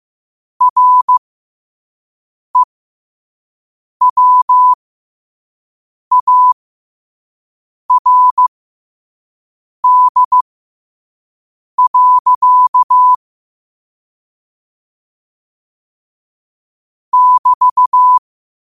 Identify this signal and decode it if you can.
Morse code, heard as a keyed tone, about 15 wpm, spaced as for 5 wpm, 1000 Hz. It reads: REWARD. <BT>
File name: qotdfiles / 5wpm